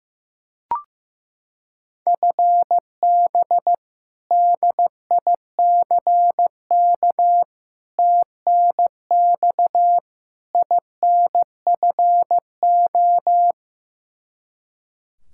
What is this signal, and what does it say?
Morse code, heard as a keyed tone, about 15 wpm, 700 Hz. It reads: FB DICK TNX INFO
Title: Practice QSO - Exchange 3 - More Info - 15wpm